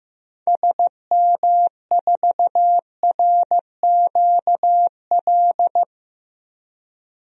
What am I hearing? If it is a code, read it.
Morse code, heard as a keyed tone, about 16 wpm, 700 Hz. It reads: SM4RQL
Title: DXCC-100 Call Signs - 6 Characters - Repeated 2x - 15wpm